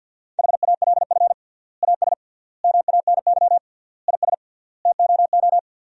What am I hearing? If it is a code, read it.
Morse code, heard as a keyed tone, about 50 wpm, 700 Hz. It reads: HALF AS MANY IS TOO
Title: 3x Word-Spacing - Sentences from Top 400 Words - 50wpm